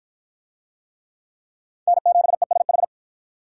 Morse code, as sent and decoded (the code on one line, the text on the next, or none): -. --... ... ....
N7ESH